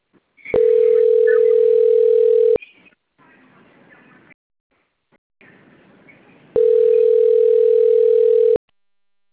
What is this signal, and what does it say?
Ringback tone (North America), 440 + 480 Hz.